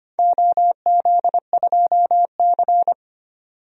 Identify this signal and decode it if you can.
Morse code, heard as a keyed tone, about 25 wpm, 700 Hz. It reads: OZ2C
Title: DXCC-100 Call Signs - 4 Characters - 25wpm